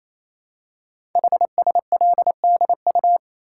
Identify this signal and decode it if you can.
Morse code, heard as a keyed tone, about 28 wpm, 700 Hz. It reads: HSLDU
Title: Code Groups - 5 Letter:Numbers - 28wpm